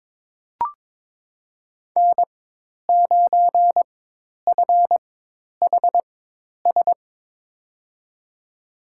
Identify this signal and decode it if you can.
Morse code, heard as a keyed tone, about 22 wpm, spaced as for 10 wpm, 700 Hz. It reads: N9FHS
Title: USA Call Signs - 5 Characters - 4x Character Spacing - 22wpm